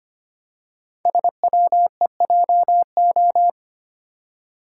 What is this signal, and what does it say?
Morse code, heard as a keyed tone, about 25 wpm, 700 Hz. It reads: SWEJO
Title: Code Groups - 5 Letters - 25wpm